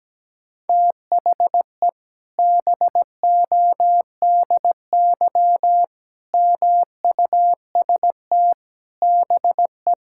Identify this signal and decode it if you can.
Morse code, heard as a keyed tone, about 17 wpm, 700 Hz. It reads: THE BODY MUST BE
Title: Sentences from Top 1000 Words - Review Part 5 - 17wpm